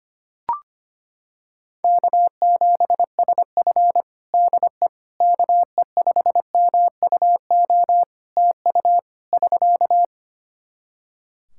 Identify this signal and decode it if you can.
Morse code, heard as a keyed tone, about 25 wpm, 700 Hz. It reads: K7SF DE KE5MUO TU <SK>
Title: K1USN SST QSO Practice 25wpm